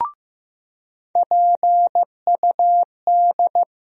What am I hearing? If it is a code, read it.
Morse code, heard as a keyed tone, about 15 wpm, 700 Hz. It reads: PUD
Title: USA Call Sign Postfixes - 3 Letters 15wpm